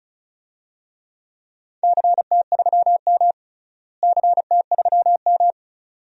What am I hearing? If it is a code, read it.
Morse code, heard as a keyed tone, about 36 wpm, 700 Hz. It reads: CT3M CT3M
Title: DXCC-100 Call Signs - 4 Characters - Repeated 2x - 35wpm